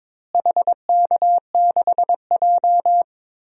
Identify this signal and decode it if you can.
Morse code, heard as a keyed tone, about 22 wpm, 700 Hz. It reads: HK6J
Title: DXCC-100 Call Signs - 4 Characters - Repeated 3x - 22wpm